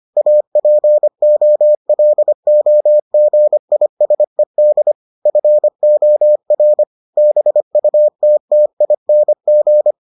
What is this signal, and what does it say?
Morse code, heard as a keyed tone, about 25 wpm, 600 Hz. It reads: APOLOGISED FOR BUTTING